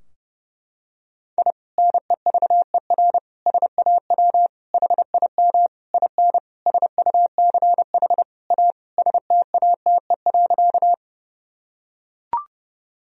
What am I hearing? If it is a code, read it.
Morse code, heard as a keyed tone, about 31 wpm, 700 Hz. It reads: I NEVER SAW HIM IN SUCH A STATE.
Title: Sentences from Top 1000 Words - Review Part 2 - 30wpm